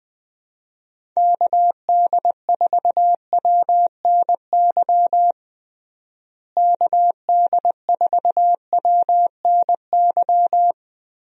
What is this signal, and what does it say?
Morse code, heard as a keyed tone, about 20 wpm, 700 Hz. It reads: KD4WNY KD4WNY
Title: USA Call Signs - 6 Characters - Repeated 2x - 20wpm